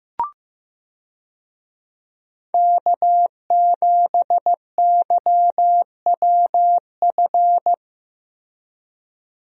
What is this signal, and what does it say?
Morse code, heard as a keyed tone, about 15 wpm, 700 Hz. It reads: K7YWF
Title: USA Call Signs - 5 Characters - Repeated 2x - 15wpm